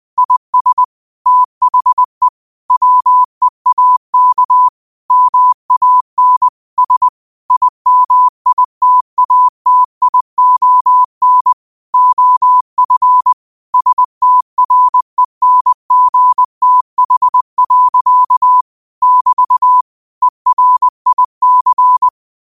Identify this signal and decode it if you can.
Morse code, heard as a keyed tone, about 20 wpm, 1000 Hz. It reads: IS THE WEAK MAN S IMITATION OF STRENGTH. <BT> ERIC